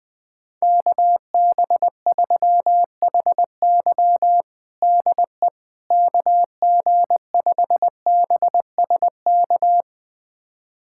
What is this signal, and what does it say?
Morse code, heard as a keyed tone, about 21 wpm, 700 Hz. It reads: KB3HY DE KG5BSK
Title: USA Callsign DE Callsign 20wpm